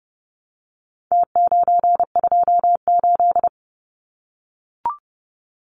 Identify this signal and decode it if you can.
Morse code, heard as a keyed tone, about 30 wpm, 700 Hz. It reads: T928
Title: Numbers - Contest Style - 4-Digit 30wpm